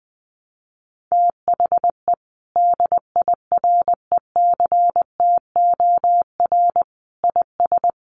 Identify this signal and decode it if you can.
Morse code, heard as a keyed tone, about 20 wpm, 700 Hz. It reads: THE DIRECTOR IS
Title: Sentences from Top 1300 Words 20wpm